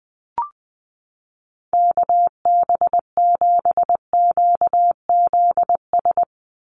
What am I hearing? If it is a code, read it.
Morse code, heard as a keyed tone, about 20 wpm, 700 Hz. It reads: KB7QZS
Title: USA Call Signs 20wpm